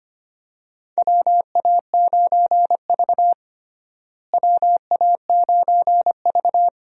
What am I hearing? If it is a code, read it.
Morse code, heard as a keyed tone, about 25 wpm, 700 Hz. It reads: WA9V WA9V
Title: USA Call Signs - 4 Characters - Repeated 2x - 25wpm